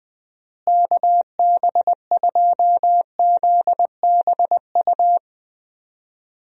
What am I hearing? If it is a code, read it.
Morse code, heard as a keyed tone, about 20 wpm, 700 Hz. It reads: KB2ZBU
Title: USA Call Signs - 6 Characters - Repeated 3x - 20wpm